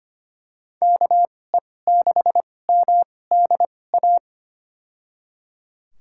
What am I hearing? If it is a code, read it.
Morse code, heard as a keyed tone, about 25 wpm, spaced as for 18 wpm, 700 Hz. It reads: KE6MDA
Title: USA Call Signs - 2x Character Spacing - 25wpm